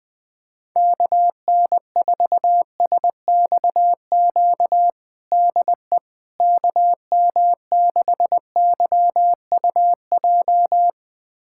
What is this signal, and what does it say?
Morse code, heard as a keyed tone, about 20 wpm, 700 Hz. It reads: KN4SXQ DE KM6YUJ
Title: USA Callsign DE Callsign 20wpm